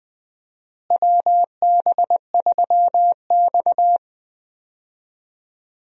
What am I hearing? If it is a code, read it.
Morse code, heard as a keyed tone, about 20 wpm, 700 Hz. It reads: WB3X
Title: USA Call Signs - 4 Characters - Repeated 3x - 20wpm